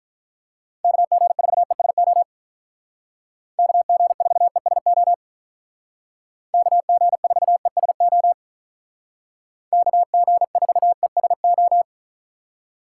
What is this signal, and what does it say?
Morse code, heard as a keyed tone, about 44 wpm, 700 Hz. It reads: KG4ESO KG4ESO KG4ESO KG4ESO
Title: Speed Racing - USA 2x3 Call Signs 35wpm